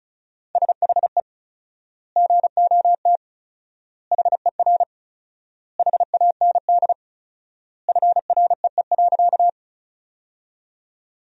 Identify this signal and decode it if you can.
Morse code, heard as a keyed tone, about 34 wpm, 700 Hz. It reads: SHE GOT HER HAND FREE.
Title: 4x Word-Spacing - Sentences from Top 500 Words - 35wpm